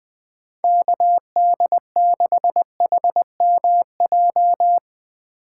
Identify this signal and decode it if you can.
Morse code, heard as a keyed tone, about 21 wpm, 700 Hz. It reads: KD6HMJ
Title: USA Call Signs - 6 Characters - 4x Character Spacing - 20wpm